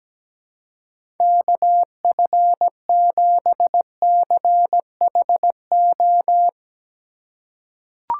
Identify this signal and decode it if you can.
Morse code, heard as a keyed tone, about 17 wpm, 700 Hz. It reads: KF7CHO
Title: USA Call Signs - 4x Character Spacing - 17wpm